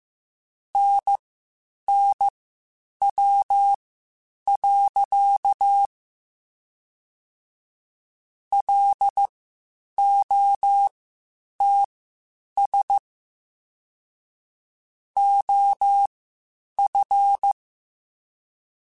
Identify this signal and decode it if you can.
Morse code, heard as a keyed tone, about 15 wpm, spaced as for 8 wpm, 800 Hz. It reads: NNW. LOTS OF
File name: Spaced Characters / Spaced Characters-mp3s / Weather 5 12 sp @13